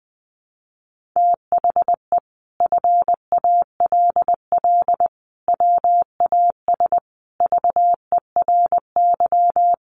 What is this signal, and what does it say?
Morse code, heard as a keyed tone, about 20 wpm, 700 Hz. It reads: THE FALL WAS VERY
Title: Sentences from Top 1400 Words 20wpm